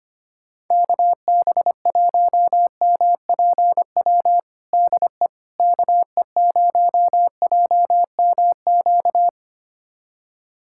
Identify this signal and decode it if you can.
Morse code, heard as a keyed tone, about 25 wpm, 700 Hz. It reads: KB1MPW DE KE0JMQ
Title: USA Callsign DE Callsign - 6 Characters - 25wpm